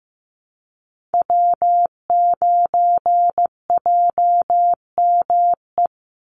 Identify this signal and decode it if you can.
Morse code, heard as a keyed tone, about 15 wpm, 700 Hz. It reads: W9JME
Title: USA Call Signs 15wpm